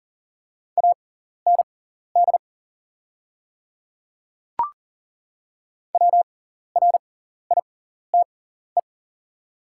Morse code, heard as a keyed tone, about 40 wpm, spaced as for 14 wpm, 700 Hz. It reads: AND WRITE
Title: 6x Character Spacing - Top 100 Words - 40wpm